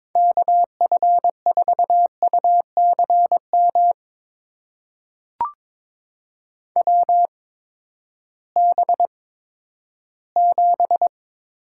Morse code, heard as a keyed tone, about 22 wpm, spaced as for 13 wpm, 700 Hz. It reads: KF4UCM WB7
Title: USA Call Signs - 6 Characters - 8x Character Spacing - 22wpm